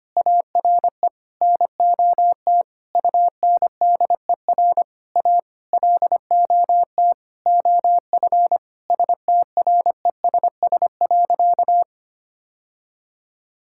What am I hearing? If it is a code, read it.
Morse code, heard as a keyed tone, about 25 wpm, 700 Hz. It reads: ARE NOT UNDER A LOT OF STRESS.
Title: Sentences from Top 1000 Words - Review Part 6 - 25wpm